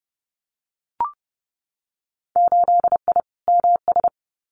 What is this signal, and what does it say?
Morse code, heard as a keyed tone, about 30 wpm, 700 Hz. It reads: 8I MS
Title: Winter Field Day Exchanges Encore 30wpm